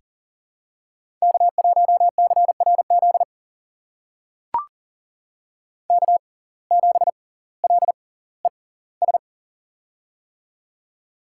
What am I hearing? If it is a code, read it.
Morse code, heard as a keyed tone, about 40 wpm, spaced as for 20 wpm, 700 Hz. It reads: K1CRZ K7LES
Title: USA Call Signs - 6x Character Spacing - 40wpm